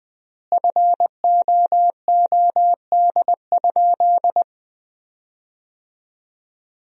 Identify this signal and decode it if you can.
Morse code, heard as a keyed tone, about 20 wpm, 700 Hz. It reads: FOOD?
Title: 4x Word-Spacing - Sentences from Top 300 Words - 20wpm